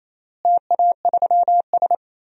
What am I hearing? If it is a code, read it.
Morse code, heard as a keyed tone, about 27 wpm, 700 Hz. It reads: TA3S
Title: DXCC-100 Call Signs - 4 Characters - Repeated 3x - 28wpm